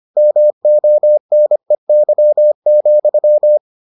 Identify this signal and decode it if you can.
Morse code, heard as a keyed tone, about 25 wpm, 600 Hz. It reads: MONEY,